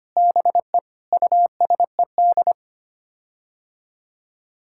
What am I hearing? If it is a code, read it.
Morse code, heard as a keyed tone, about 25 wpm, 700 Hz. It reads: BE USED
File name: Sets of 9 Words - 25wpm